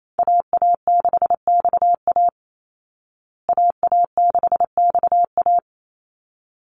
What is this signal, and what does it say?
Morse code, heard as a keyed tone, about 28 wpm, 700 Hz. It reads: AA6XA AA6XA
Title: Call Signs Repeated 3x - 28wpm